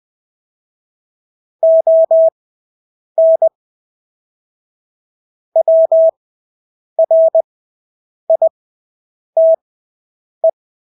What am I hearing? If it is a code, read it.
Morse code, heard as a keyed tone, about 20 wpm, spaced as for 8 wpm, 650 Hz. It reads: ON WRITE
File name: Engelske ord 20/8 wpm